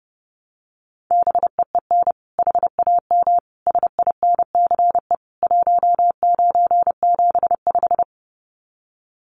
Morse code, heard as a keyed tone, about 30 wpm, 700 Hz. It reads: BEEN HAM SINCE 1975
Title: Common QSO Phrases 30wpm